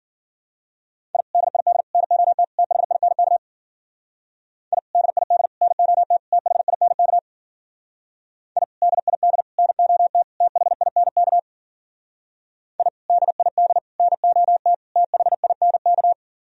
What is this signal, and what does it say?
Morse code, heard as a keyed tone, about 48 wpm, 700 Hz. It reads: I DID NOT THINK I DID NOT THINK I DID NOT THINK I DID NOT THINK
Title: Speed-Racing - Sets of 4 Words 40wpm